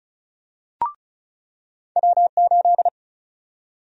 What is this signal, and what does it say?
Morse code, heard as a keyed tone, about 34 wpm, 700 Hz. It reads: W8
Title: USA Call Sign Prefixes - 35wpm